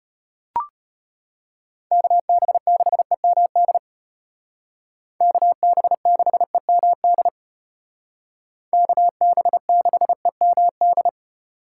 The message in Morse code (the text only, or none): KB6EMD KB6EMD KB6EMD